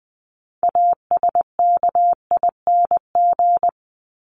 Morse code, heard as a keyed tone, about 20 wpm, 700 Hz. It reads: ASKING
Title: ING Suffix - Top 100 Words with Base Words 20wpm